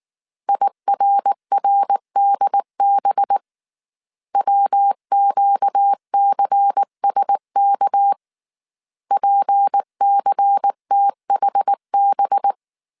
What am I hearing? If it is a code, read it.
Morse code, heard as a keyed tone, about 19 wpm, 800 Hz. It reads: IRRDB WQCSK PCTHB